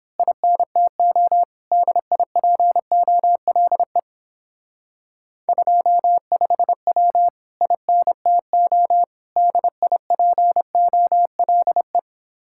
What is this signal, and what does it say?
Morse code, heard as a keyed tone, about 26 wpm, 700 Hz. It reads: INTO DIPOLE 25W INTO DIPOLE
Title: Speed-Racing Common QSO Phrases - 22wpm